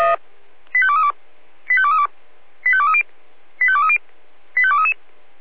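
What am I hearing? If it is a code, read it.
Five-tone selective call (ZVEI), heard as quick runs of five tones, tones of about 70 ms. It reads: 85232 85232 85230 85230 85233
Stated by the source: ZVEI five-tone selective call received on an SDR.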